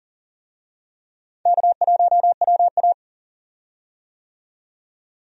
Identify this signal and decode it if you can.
Morse code, heard as a keyed tone, about 40 wpm, 700 Hz. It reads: K1WA
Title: USA Call Signs - 4 Characters - 40wpm